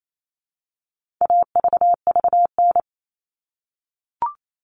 Morse code, heard as a keyed tone, about 28 wpm, 700 Hz. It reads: AVVN
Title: All Cut Numbers - 4-Digit 28wpm